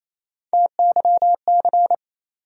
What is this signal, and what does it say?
Morse code, heard as a keyed tone, about 28 wpm, 700 Hz. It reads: TYC
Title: USA Call Sign Postfixes - 3 Letters 28wpm